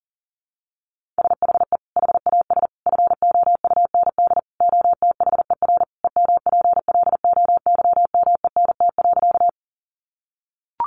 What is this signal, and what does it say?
Morse code, heard as a keyed tone, about 40 wpm, 700 Hz. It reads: SHE HAS FOUND OTHER EMPLOYMENT.
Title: Sentences from Top 1500 Words 40wpm